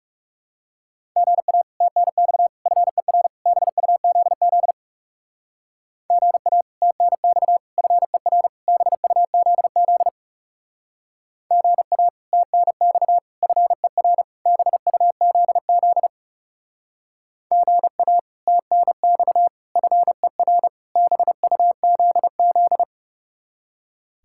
Morse code, heard as a keyed tone, about 40 wpm, 700 Hz. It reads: GA TNX FER BUZZ GA TNX FER BUZZ GA TNX FER BUZZ GA TNX FER BUZZ